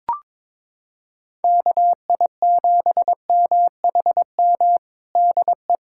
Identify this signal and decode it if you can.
Morse code, heard as a keyed tone, about 22 wpm, 700 Hz. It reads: KI7MHM DE
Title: USA Callsign DE Callsign - 6 Characters - 22wpm